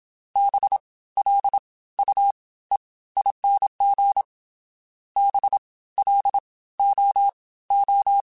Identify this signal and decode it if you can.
Morse code, heard as a keyed tone, about 27 wpm, 800 Hz. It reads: B L U E ING B L O O